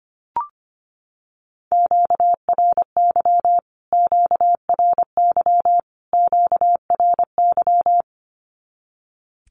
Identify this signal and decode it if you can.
Morse code, heard as a keyed tone, about 25 wpm, 700 Hz. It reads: QRY QRY QRY
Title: Q-Codes Repeated 3x 25wpm